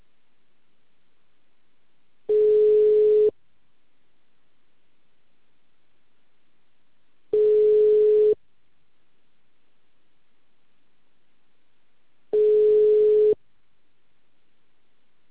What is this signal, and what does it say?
Ringback tone (Europe, 425 Hz).